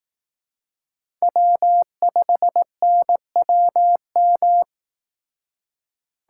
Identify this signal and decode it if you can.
Morse code, heard as a keyed tone, about 18 wpm, 700 Hz. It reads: W5NWM
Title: Speed Racing - USA 1x3 Call Signs 15wpm